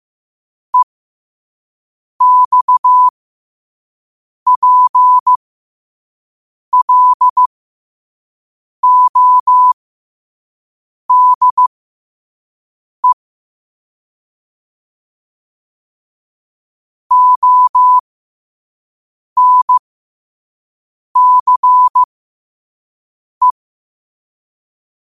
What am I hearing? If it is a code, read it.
Morse code, heard as a keyed tone, about 15 wpm, spaced as for 5 wpm, 1000 Hz. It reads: EXPLODE ONCE
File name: qotdfiles / 5wpm